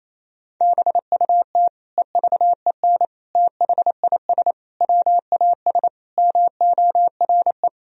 Morse code, heard as a keyed tone, about 28 wpm, 700 Hz. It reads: BUT EVEN THIS WAS MORE
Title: Sentences from Top 1000 Words - Review Part 2 - 28wpm